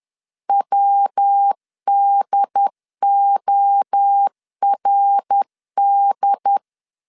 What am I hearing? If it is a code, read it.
Morse code, heard as a keyed tone, about 10 wpm, 800 Hz. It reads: WDORD